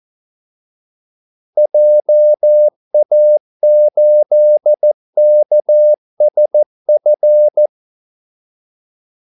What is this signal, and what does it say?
Morse code, heard as a keyed tone, about 14 wpm, 600 Hz. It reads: JA8KSF
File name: Callsigns-14wpm